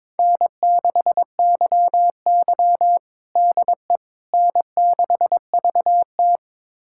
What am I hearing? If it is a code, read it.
Morse code, heard as a keyed tone, about 22 wpm, 700 Hz. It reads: N6YY DE N6VT
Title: USA Callsign DE Callsign - 4 Characters - 22wpm